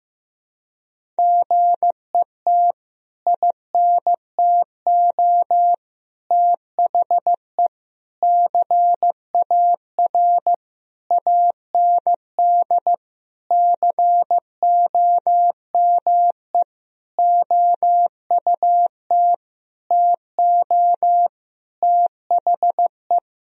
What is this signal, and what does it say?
Morse code, heard as a keyed tone, about 15 wpm, 700 Hz. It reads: GET INTO THE CAR AND COME OUT TO THE
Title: Sentences from Top 1000 Words - Review Part 1 - 15wpm